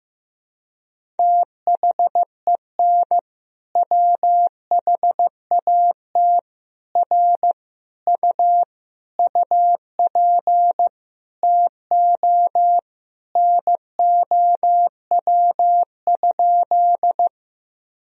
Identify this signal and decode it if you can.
Morse code, heard as a keyed tone, about 16 wpm, 700 Hz. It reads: THEN WHAT R U UP TO NOW?
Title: Sentences from Top 100 Words and Abbreviations 15wpm